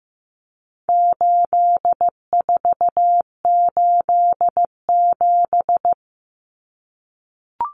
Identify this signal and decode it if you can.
Morse code, heard as a keyed tone, about 15 wpm, 700 Hz. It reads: 8487
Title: Common Cut Numbers - 4-Digit 15wpm